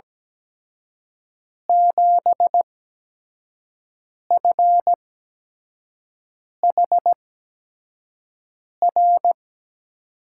Morse code, heard as a keyed tone, about 17 wpm, spaced as for 5 wpm, 700 Hz. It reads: 7FHR
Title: USA Call Signs - 8x Character Spacing - 17wpm